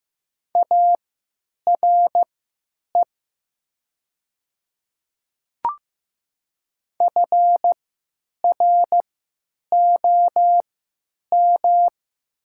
Morse code, heard as a keyed tone, about 15 wpm, spaced as for 9 wpm, 700 Hz. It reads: ARE FROM
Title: 3x Character Spacing - Top 100 Words - 15wpm